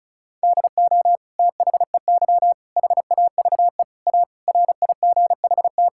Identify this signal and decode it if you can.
Morse code, heard as a keyed tone, about 34 wpm, 700 Hz. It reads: DO THEY HAVE A RIGHT
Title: Sentences from Top 1000 Words - Review Part 6 - 35wpm